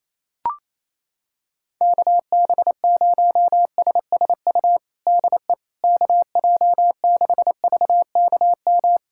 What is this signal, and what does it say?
Morse code, heard as a keyed tone, about 28 wpm, 700 Hz. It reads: KB0SSU DE KJ6VKM
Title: USA Callsign DE Callsign - 6 Characters - 28wpm